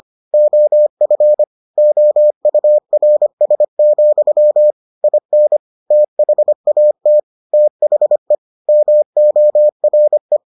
Morse code, heard as a keyed tone, about 25 wpm, 600 Hz. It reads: OF OURS, IN THAT THE MORE